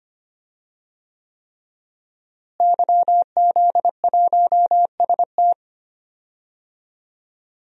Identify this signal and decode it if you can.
Morse code, heard as a keyed tone, about 25 wpm, 700 Hz. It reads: YZ1ST
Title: DXCC-100 Call Signs - 5 Characters - Repeated 3x - 25wpm